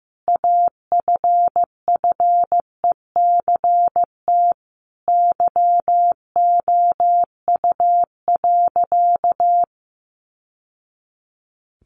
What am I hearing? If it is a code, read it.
Morse code, heard as a keyed tone, about 14 wpm, 700 Hz. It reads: AFFECT YOU.
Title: Sentences from Top 900 Words 15wpm